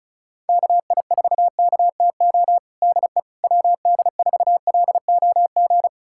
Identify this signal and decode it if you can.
Morse code, heard as a keyed tone, about 34 wpm, 700 Hz. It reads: KI4KTO DE WD4LOG
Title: USA Callsign DE Callsign - 6 Characters - 35wpm